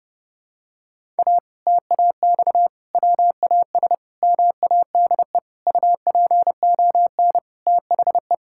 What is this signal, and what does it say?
Morse code, heard as a keyed tone, about 32 wpm, 700 Hz. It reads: A TAX WAS MADE UPON THE
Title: Sentences from Top 1000 Words - Review Part 6 - 30wpm